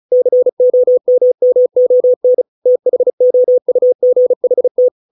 Morse code, heard as a keyed tone, about 34 wpm, 500 Hz. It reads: COMMON THOUGHT
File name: 632 words / 632 Words 35WPM